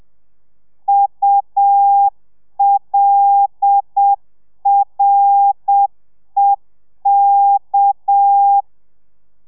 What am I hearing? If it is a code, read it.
Morse code, heard as a keyed tone, about 7 wpm, 800 Hz. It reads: ULREK